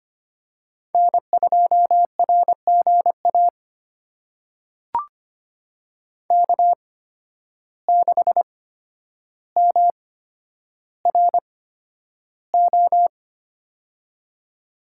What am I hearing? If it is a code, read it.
Morse code, heard as a keyed tone, about 25 wpm, spaced as for 8 wpm, 700 Hz. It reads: N2RG AK6MRO
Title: USA Call Signs - 8x Character Spacing - 25wpm